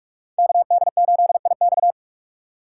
Morse code, heard as a keyed tone, about 45 wpm, 690 Hz. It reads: KD8IX
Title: USA Call Signs - 5 Characters - 2x Character Spacing - 45wpm